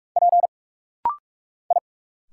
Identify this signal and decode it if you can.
Morse code, heard as a keyed tone, about 46 wpm, 700 Hz. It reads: P I
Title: Single Letters - Rapid-Fire 45wpm